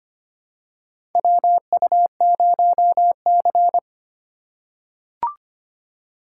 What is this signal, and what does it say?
Morse code, heard as a keyed tone, about 25 wpm, 700 Hz. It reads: WU0C
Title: USA Call Signs - 4 Characters - 6x Character Spacing - 25wpm